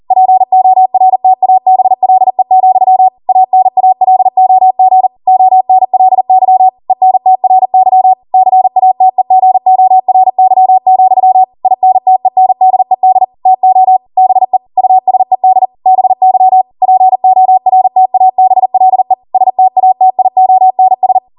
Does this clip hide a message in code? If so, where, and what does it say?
Yes: Morse code from 0.1 to 21.2 s, heard as a keyed tone, about 40 wpm, 750 Hz. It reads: PORTABLE, ANALOG ONLY ENTRY CATEGORY, INTENDED TO BE USED BY PORTABLE STATIONS